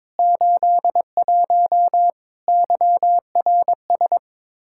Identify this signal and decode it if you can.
Morse code, heard as a keyed tone, about 22 wpm, 700 Hz. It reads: 81 YRS
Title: Practice QSO - Exchange 3 - More Info - 22wpm